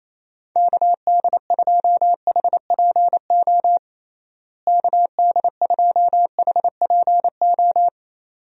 Morse code, heard as a keyed tone, about 28 wpm, 700 Hz. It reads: KD2HPO KD2HPO
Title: USA Call Signs - 6 Characters - Repeated 2x - 28wpm